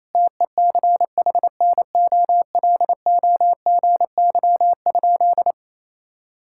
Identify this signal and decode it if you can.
Morse code, heard as a keyed tone, about 28 wpm, 700 Hz. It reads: TECHNOLOGY?
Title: Sentences from Top 1000 Words - Review Part 5 - 28wpm